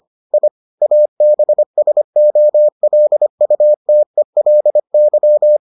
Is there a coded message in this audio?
Yes: Morse code from 0.3 to 5.6 s, heard as a keyed tone, about 25 wpm, 600 Hz. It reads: I ABSOLUTELY